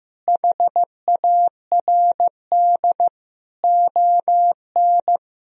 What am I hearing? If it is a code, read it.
Morse code, heard as a keyed tone, about 15 wpm, 700 Hz. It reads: HARD ON
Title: Sentences from Top 1000 Words - Review Part 1 - 15wpm